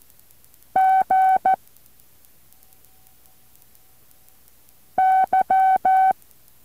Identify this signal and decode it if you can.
Morse code, heard as a keyed tone, about 14 wpm, 760 Hz. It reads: G Y